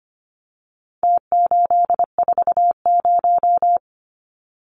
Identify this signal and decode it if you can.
Morse code, heard as a keyed tone, about 25 wpm, 700 Hz. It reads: T840
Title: Numbers - Contest Style - 4-Digit 25wpm